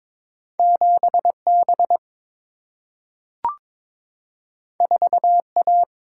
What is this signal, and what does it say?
Morse code, heard as a keyed tone, about 22 wpm, 700 Hz. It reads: 7 B 4 A
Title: Single-Letter Single-Number 22wpm